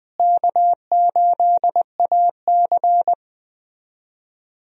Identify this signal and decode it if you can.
Morse code, heard as a keyed tone, about 20 wpm, 700 Hz. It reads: K8AC
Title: USA Call Signs - 4 Characters - Repeated 2x - 20wpm